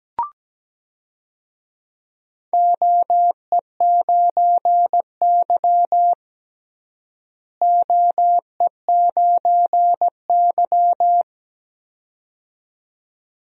Morse code, heard as a keyed tone, about 17 wpm, 700 Hz. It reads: OE9Y OE9Y
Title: DXCC-100 Call Signs - 4 Characters - Repeated 2x - 17wpm